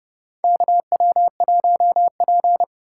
Morse code, heard as a keyed tone, about 32 wpm, 700 Hz. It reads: KW1P
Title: USA Call Signs - 2x Character Spacing - 30wpm